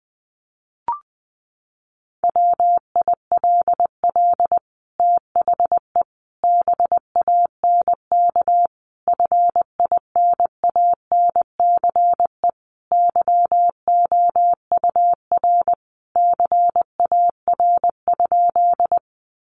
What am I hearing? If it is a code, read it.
Morse code, heard as a keyed tone, about 20 wpm, 700 Hz. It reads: WILL THE BANK FINANCE YOUR CAR?
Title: Sentences from Top 1500 Words 20wpm